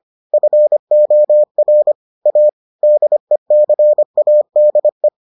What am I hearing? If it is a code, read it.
Morse code, heard as a keyed tone, about 25 wpm, 600 Hz. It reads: FOR A DECADE